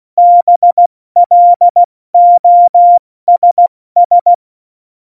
Morse code, heard as a keyed tone, about 16 wpm, 700 Hz. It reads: BLOSS
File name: Morsekurs / Witze / Witz 04. Dezember, 70 F80